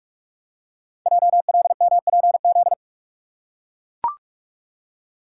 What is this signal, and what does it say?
Morse code, heard as a keyed tone, about 44 wpm, 690 Hz. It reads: JLMPZ
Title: Code Groups - 5 Letters - 45wpm